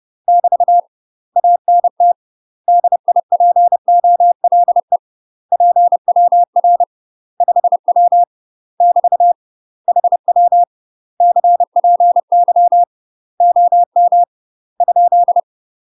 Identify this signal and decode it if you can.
Morse code, heard as a keyed tone, about 30 wpm, 700 Hz. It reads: <BT> ANT DIPOLE PWR 5W <BT> HW CPY OM ?
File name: UK MASTER QSOs TRAINING / 30WPM / UK MASTER QSOs 0005